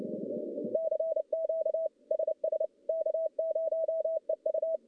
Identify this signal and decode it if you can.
Morse code, heard as a keyed tone, about 30 wpm, 650 Hz. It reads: CQ SS K0EU
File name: contests / CW-SS-k5tr- -1100